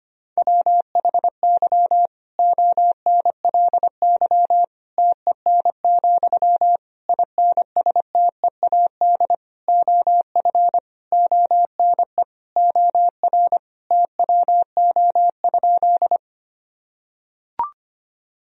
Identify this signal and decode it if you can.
Morse code, heard as a keyed tone, about 25 wpm, 700 Hz. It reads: WHY ONLY TEN, INSTEAD OF ONE OR TWO?
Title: Sentences from Top 1000 Words - Review Part 4 - 25wpm